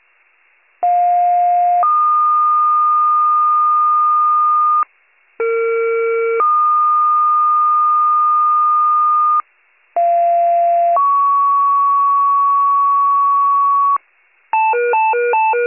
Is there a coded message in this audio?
Yes: two-tone page from 0.8 to 14.0 s, heard as a one-second tone, then a three-second tone, about 710 then 1220 Hz, about 460 then 1220 Hz, about 690 then 1120 Hz.